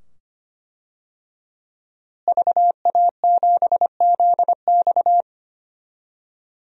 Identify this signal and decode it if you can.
Morse code, heard as a keyed tone, about 25 wpm, 700 Hz. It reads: VA7ZX
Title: DXCC-100 Call Signs - 5 Characters - Repeated 2x - 25wpm